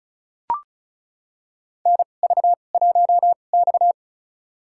Morse code, heard as a keyed tone, about 34 wpm, spaced as for 25 wpm, 700 Hz. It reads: NV1X
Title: USA Call Signs - 4 Characters - 2x Character Spacing - 35wpm